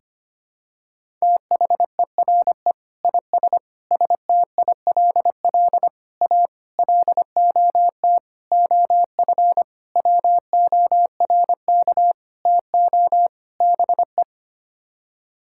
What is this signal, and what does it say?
Morse code, heard as a keyed tone, about 25 wpm, 700 Hz. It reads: THERE IS STILL A LOT OF WORK TO BE
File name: Sets of 9 Words - 25wpm